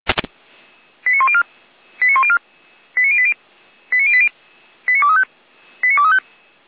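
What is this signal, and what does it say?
Five-tone selective call (ZVEI), heard as quick runs of five tones, tones of about 70 ms. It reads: 89104 89104 89080 89080 89235 89235
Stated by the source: ZVEI five-tone selective call received on an SDR.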